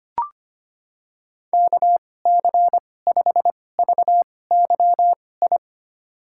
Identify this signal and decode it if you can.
Morse code, heard as a keyed tone, about 25 wpm, spaced as for 18 wpm, 700 Hz. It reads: KC5VYI
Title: USA Call Signs - 6 Characters - 2x Character Spacing - 25wpm